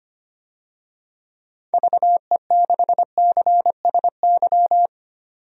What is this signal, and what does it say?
Morse code, heard as a keyed tone, about 25 wpm, 700 Hz. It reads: VE6CSY
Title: DXCC-100 Call Signs - 6 Characters - 25wpm